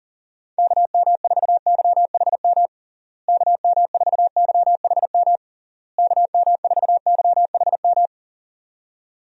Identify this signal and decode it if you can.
Morse code, heard as a keyed tone, about 37 wpm, 700 Hz. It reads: KM4YHM KM4YHM KM4YHM
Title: USA Call Signs - 6 Characters - Repeated 3x - 40wpm